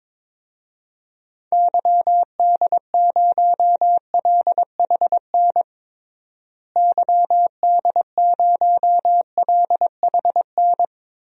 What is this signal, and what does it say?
Morse code, heard as a keyed tone, about 22 wpm, 700 Hz. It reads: YD0LHN YD0LHN
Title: DXCC-100 Call Signs - 6 Characters - Repeated 2x - 22wpm